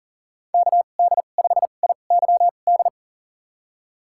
Morse code, heard as a keyed tone, about 40 wpm, spaced as for 29 wpm, 690 Hz. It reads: KD5IYD
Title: USA Call Signs - 6 Characters - 2x Character Spacing - 40wpm